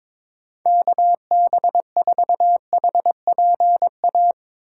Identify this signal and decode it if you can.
Morse code, heard as a keyed tone, about 22 wpm, 700 Hz. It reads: KB4HPA
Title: USA Call Signs - 6 Characters - 3x Character Spacing - 22wpm